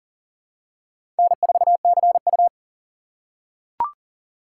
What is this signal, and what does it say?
Morse code, heard as a keyed tone, about 40 wpm, 700 Hz. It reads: N4CU
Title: USA Call Signs - 3x Character Spacing - 40wpm